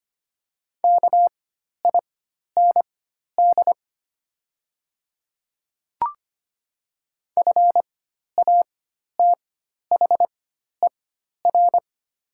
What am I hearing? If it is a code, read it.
Morse code, heard as a keyed tone, about 25 wpm, spaced as for 12 wpm, 700 Hz. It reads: KIND FATHER
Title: 4x Character Spacing - Top 200 Words - 25wpm